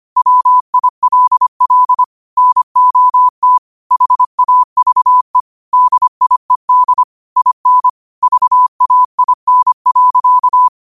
Morse code, heard as a keyed tone, about 25 wpm, 1000 Hz. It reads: WILL NOT HAVE DIED IN VAIN.